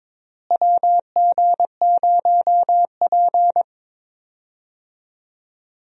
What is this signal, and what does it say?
Morse code, heard as a keyed tone, about 22 wpm, 700 Hz. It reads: WG0P
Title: USA Call Signs - 4 Characters - Repeated 3x - 22wpm